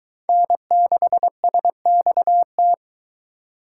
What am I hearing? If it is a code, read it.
Morse code, heard as a keyed tone, about 24 wpm, 700 Hz. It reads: N6SXT